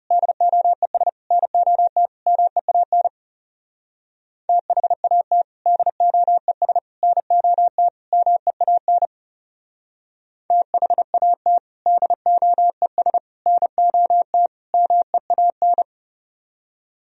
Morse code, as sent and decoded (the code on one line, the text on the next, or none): -.. --- . ... / -. --- - / -- . .- -. / - .... .- - / -.. --- . ... / -. --- - / -- . .- -. / - .... .- - / -.. --- . ... / -. --- - / -- . .- -.
DOES NOT MEAN THAT DOES NOT MEAN THAT DOES NOT MEAN